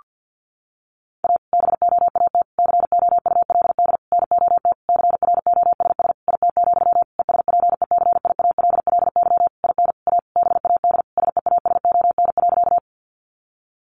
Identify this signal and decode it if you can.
Morse code, heard as a keyed tone, about 50 wpm, 700 Hz. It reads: A BOAT COULD NOT CROSS IT, ESPECIALLY IN A BAD SEASON.
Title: Sentences from Top 1400 Words 50wpm